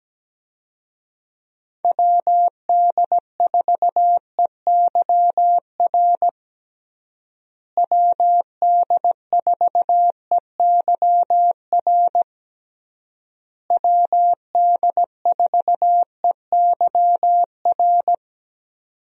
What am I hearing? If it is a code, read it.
Morse code, heard as a keyed tone, about 17 wpm, 700 Hz. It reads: WD4EYR WD4EYR WD4EYR
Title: USA Call Signs - 6 Characters - Repeated 3x - 17wpm